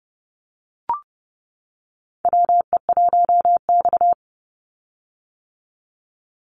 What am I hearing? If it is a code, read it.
Morse code, heard as a keyed tone, about 29 wpm, 700 Hz. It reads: WE1X
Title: USA Call Signs 30wpm